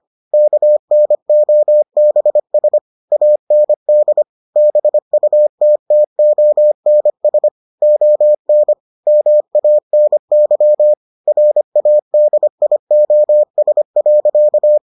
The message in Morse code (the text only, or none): KNOBS AND BUTTONS ON MANY RADIOS.